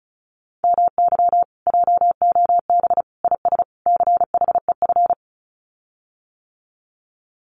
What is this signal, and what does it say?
Morse code, heard as a keyed tone, about 34 wpm, 700 Hz. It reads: MY JOB IS CHEF
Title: Common QSO Phrases Encore 35wpm